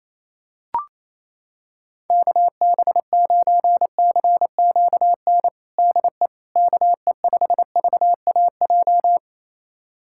Morse code, heard as a keyed tone, about 28 wpm, 700 Hz. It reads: KB9CQN DE KE5VAJ
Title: USA Callsign DE Callsign - 6 Characters - 28wpm